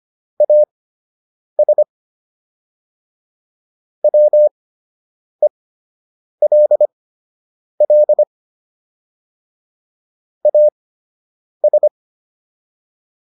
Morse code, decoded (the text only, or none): AS WELL AS